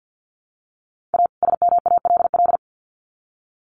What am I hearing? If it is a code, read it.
Morse code, heard as a keyed tone, about 50 wpm, 700 Hz. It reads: A SMALL
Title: Sets of 2 Words 50wpm